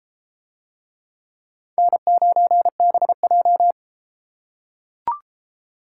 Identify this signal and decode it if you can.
Morse code, heard as a keyed tone, about 32 wpm, 700 Hz. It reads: N9BJ